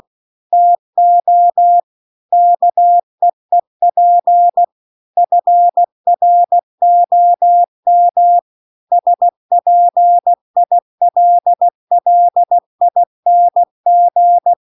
Morse code, heard as a keyed tone, about 16 wpm, 700 Hz. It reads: TO KEEP FROM SPILLING